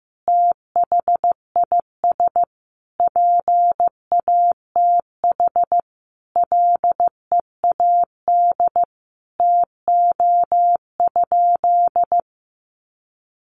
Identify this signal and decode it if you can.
Morse code, heard as a keyed tone, about 14 wpm, 700 Hz. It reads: THIS PATH LEAD TO?
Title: Sentences from Top 1400 Words 15wpm